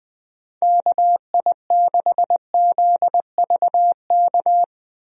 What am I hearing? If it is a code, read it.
Morse code, heard as a keyed tone, about 20 wpm, 700 Hz. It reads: KI6ZVK
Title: USA Call Signs - 6 Characters - Repeated 3x - 20wpm